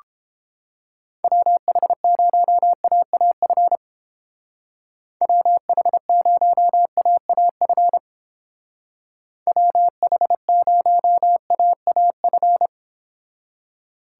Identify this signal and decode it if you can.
Morse code, heard as a keyed tone, about 29 wpm, 700 Hz. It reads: WH0AAF WH0AAF WH0AAF